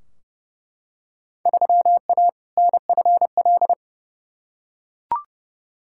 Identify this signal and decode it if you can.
Morse code, heard as a keyed tone, about 30 wpm, 700 Hz. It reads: 3A NFL
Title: Field Day Exchanges - 30wpm